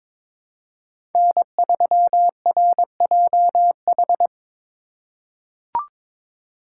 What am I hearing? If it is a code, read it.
Morse code, heard as a keyed tone, about 22 wpm, 700 Hz. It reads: N3RJH
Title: USA Call Signs - 5 Characters - 4x Character Spacing - 22wpm